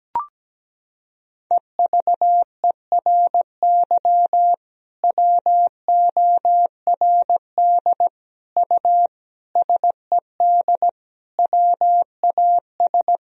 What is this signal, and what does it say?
Morse code, heard as a keyed tone, about 17 wpm, 700 Hz. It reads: EVERY WORD U SED WAS
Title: Sentences from Top 400 Words and Abbreviations 17wpm